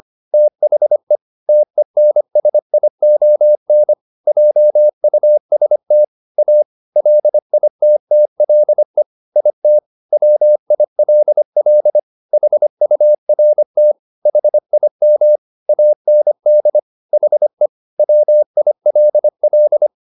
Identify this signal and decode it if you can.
Morse code, heard as a keyed tone, about 25 wpm, 600 Hz. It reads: THE TENSION JUST A LITTLE IT WILL HURT HIM AND HE WILL
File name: the old man and the sea part 1 25wpm